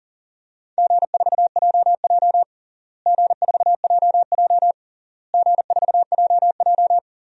Morse code, heard as a keyed tone, about 40 wpm, 700 Hz. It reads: G4JJ G4JJ G4JJ
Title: DXCC-100 Call Signs - 4 Characters - Repeated 3x - 40wpm